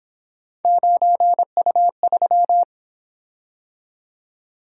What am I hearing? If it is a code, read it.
Morse code, heard as a keyed tone, about 27 wpm, 700 Hz. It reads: 9U3
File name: Speed-Racing - Three Letter-Number Combinations - 22wpm